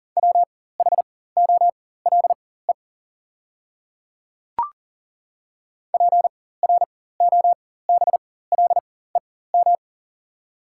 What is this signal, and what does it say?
Morse code, heard as a keyed tone, about 40 wpm, spaced as for 19 wpm, 690 Hz. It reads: WHOLE PROBLEM
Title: 4x Character Spacing - Top 500 Words - 40wpm